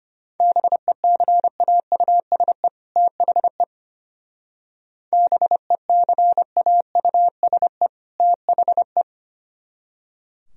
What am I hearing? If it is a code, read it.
Morse code, heard as a keyed tone, about 32 wpm, 700 Hz. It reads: BECAUSE THE BECAUSE THE